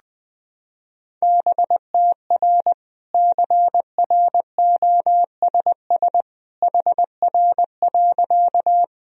Morse code, heard as a keyed tone, about 21 wpm, 700 Hz. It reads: BTR CROSS HR.
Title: Sentences from Top 400 Words and Abbreviations 20wpm